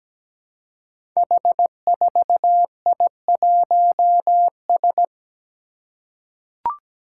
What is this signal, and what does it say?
Morse code, heard as a keyed tone, about 17 wpm, 700 Hz. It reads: H4I1S
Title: Five Challenging Letter-Number Combinations 17wpm